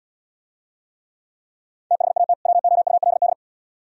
Morse code, heard as a keyed tone, about 76 wpm, 700 Hz. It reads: THAT WOULD